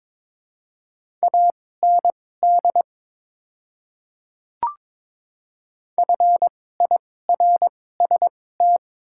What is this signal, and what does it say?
Morse code, heard as a keyed tone, about 22 wpm, spaced as for 16 wpm, 700 Hz. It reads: AND FIRST
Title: 2x Character Spacing - Top 300 Words - 22wpm